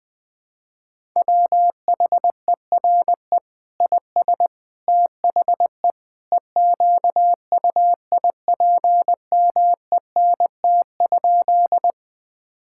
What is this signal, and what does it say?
Morse code, heard as a keyed tone, about 20 wpm, 700 Hz. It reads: WHERE IS THE EQUIPMENT?
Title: Sentences from Top 1000 Words - Review Part 5 - 20wpm